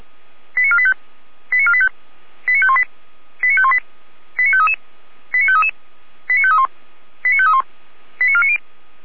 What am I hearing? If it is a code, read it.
Five-tone selective call (ZVEI), heard as quick runs of five tones, tones of about 70 ms. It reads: 89485 89485 89518 89518 89533 89533 89531 89531 89490